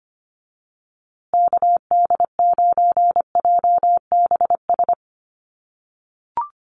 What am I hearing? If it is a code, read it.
Morse code, heard as a keyed tone, about 25 wpm, 700 Hz. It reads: KD9JBS
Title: USA Call Signs Repeated 3x 25wpm